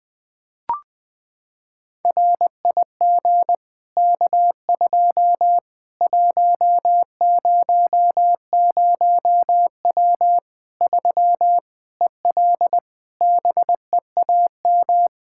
Morse code, heard as a keyed tone, about 21 wpm, 700 Hz. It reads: RIG K2 100W 3 EL BEAM
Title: K1USN SST QSO Practice 20wpm